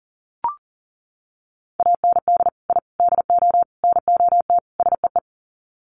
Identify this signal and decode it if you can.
Morse code, heard as a keyed tone, about 40 wpm, 700 Hz. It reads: AND I DO NOT SEE
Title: Sets of 5 Words Encore2 40wpm